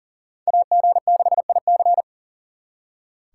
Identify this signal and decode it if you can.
Morse code, heard as a keyed tone, about 52 wpm, 700 Hz. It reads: AG6IC